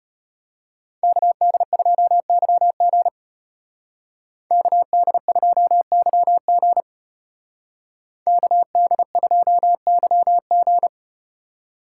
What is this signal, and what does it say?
Morse code, heard as a keyed tone, about 34 wpm, 700 Hz. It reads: KD2YG KD2YG KD2YG